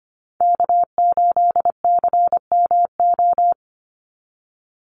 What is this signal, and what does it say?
Morse code, heard as a keyed tone, about 25 wpm, 700 Hz. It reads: K8CMO
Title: Call Signs - 25wpm